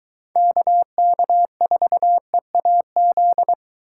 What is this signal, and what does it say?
Morse code, heard as a keyed tone, about 22 wpm, 700 Hz. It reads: KK4EAZ